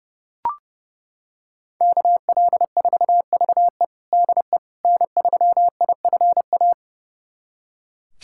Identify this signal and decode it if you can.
Morse code, heard as a keyed tone, about 30 wpm, 700 Hz. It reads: KL4VE DE N3IFA
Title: USA Callsign DE Callsign - 5 Characters - 30wpm